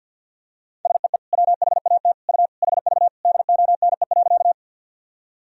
Morse code, heard as a keyed tone, about 52 wpm, 700 Hz. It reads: SEE WHAT U HV DONE.
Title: Sentences from Top 500 Words and Abbreviations 50wpm